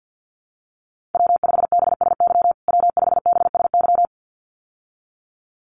Morse code, heard as a keyed tone, about 50 wpm, 700 Hz. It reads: W5BSY W5BSY
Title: USA Call Signs Repeated 2x 50wpm